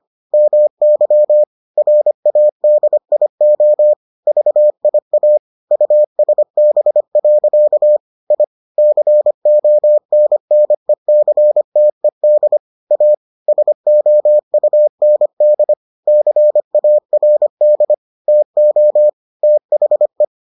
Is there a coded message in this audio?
Yes: Morse code from 0.3 to 20.2 s, heard as a keyed tone, about 25 wpm, 600 Hz. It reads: MY RADIO VIA USB. I CONNECTED A SOUND CARD TO THE